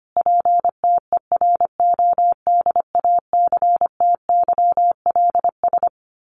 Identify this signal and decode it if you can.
Morse code, heard as a keyed tone, about 25 wpm, 700 Hz. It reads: PTERODACTYLS
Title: A Sound Of Thunder 25wpm